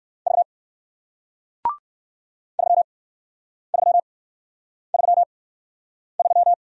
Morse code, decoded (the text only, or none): V 3333